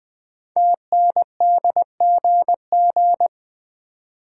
Morse code, heard as a keyed tone, about 20 wpm, 700 Hz. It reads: TNDGG
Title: Code Groups - 5 Letters - 20wpm